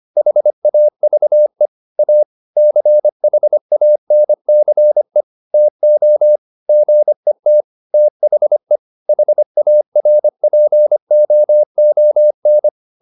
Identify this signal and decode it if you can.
Morse code, heard as a keyed tone, about 25 wpm, 600 Hz. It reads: HAVE A CHANCE TO GET THE HARPOON